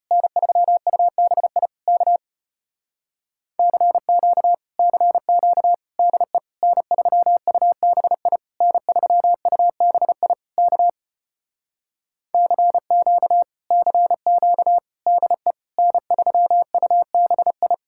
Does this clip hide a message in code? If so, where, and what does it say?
Yes: Morse code from 0.1 to 17.7 s, heard as a keyed tone, about 33 wpm, 700 Hz. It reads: N3UBI K CQ CQ DE N3UBI N3UBI K CQ CQ DE N3UBI